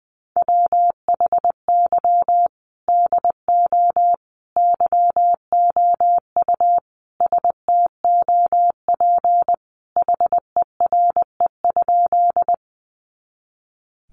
Morse code, heard as a keyed tone, about 20 wpm, 700 Hz. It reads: WHY DO YOU STOP HERE?
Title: Sentences from Top 300 Words 20wpm